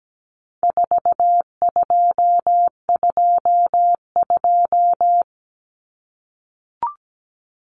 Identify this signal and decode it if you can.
Morse code, heard as a keyed tone, about 17 wpm, 700 Hz. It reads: 4222
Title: Common Cut Numbers - 4-Digit 17wpm